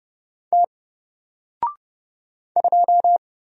Morse code, heard as a keyed tone, about 30 wpm, 700 Hz. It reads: T 2
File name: Single-Letter-Number-30wpm